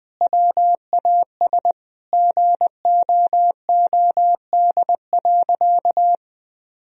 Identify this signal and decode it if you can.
Morse code, heard as a keyed tone, about 19 wpm, 700 Hz. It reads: WAS GOOD.
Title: Sentences from Top 1000 Words - Review Part 4 - 20wpm